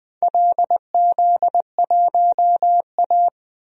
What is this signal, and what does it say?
Morse code, heard as a keyed tone, about 20 wpm, 700 Hz. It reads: LZ1A
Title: DXCC-100 Call Signs - 4 Characters - 20wpm